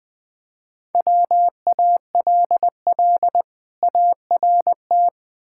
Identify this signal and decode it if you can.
Morse code, heard as a keyed tone, about 22 wpm, 700 Hz. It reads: WALL ART
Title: Difficult Words - Speed-Racing with Hints - 20wpm